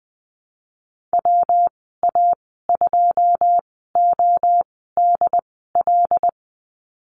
Morse code, heard as a keyed tone, about 20 wpm, spaced as for 15 wpm, 700 Hz. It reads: WA2ODL
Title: 140 Lesson 23g - Letter f - Call Signs 20wpm